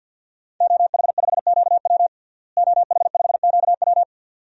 Morse code, heard as a keyed tone, about 47 wpm, 700 Hz. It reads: OH5QW OH5QW
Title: DXCC-100 Call Signs - 5 Characters - Repeated 2x - 50wpm